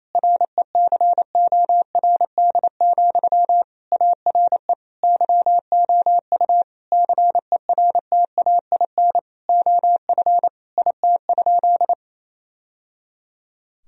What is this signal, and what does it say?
Morse code, heard as a keyed tone, about 28 wpm, 700 Hz. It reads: RECORD, ARE YOU CERTAIN OF IT?
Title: Sentences from Top 1000 Words - Review Part 3 - 28wpm